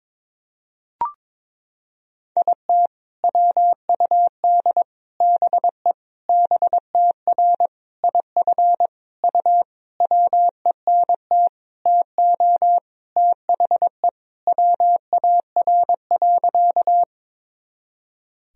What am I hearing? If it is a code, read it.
Morse code, heard as a keyed tone, about 21 wpm, 700 Hz. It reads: IT WUD BE BTR IF U WENT TO THE WAR.
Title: Sentences from Top 400 Words and Abbreviations 22wpm